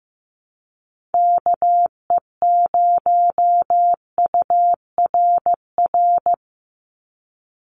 Morse code, heard as a keyed tone, about 15 wpm, 700 Hz. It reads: KE0URR
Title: USA Call Signs 15wpm